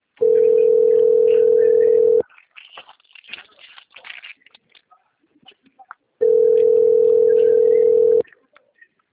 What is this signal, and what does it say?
Ringback tone (North America), 440 + 480 Hz.
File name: out- - - - - .88703